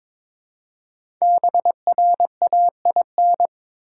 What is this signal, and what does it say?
Morse code, heard as a keyed tone, about 22 wpm, 700 Hz. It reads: BRAIN
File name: 5 Letter Words - First 3 Letters Given - 22wpm